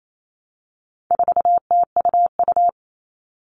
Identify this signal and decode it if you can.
Morse code, heard as a keyed tone, about 28 wpm, 700 Hz. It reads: 4TUU
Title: All Cut Numbers - 4-Digit 28wpm